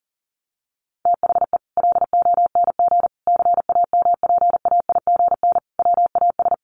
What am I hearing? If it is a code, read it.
Morse code, heard as a keyed tone, about 40 wpm, 700 Hz. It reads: THE LONG CAMPAIGN WAS
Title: Sentences from Top 1200 Words 40wpm